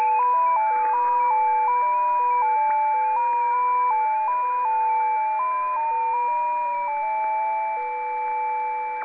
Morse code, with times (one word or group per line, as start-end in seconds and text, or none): none